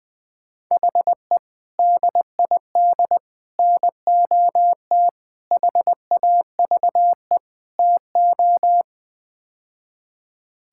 Morse code, heard as a keyed tone, about 20 wpm, 700 Hz. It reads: HE DID NOT HAVE TO
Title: Build A Phrase - 5-Word Phrases - 20wpm